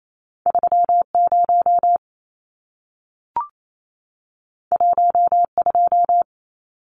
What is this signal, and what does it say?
Morse code, heard as a keyed tone, about 28 wpm, 700 Hz. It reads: 30 12
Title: Numbers - 2-Digit 28wpm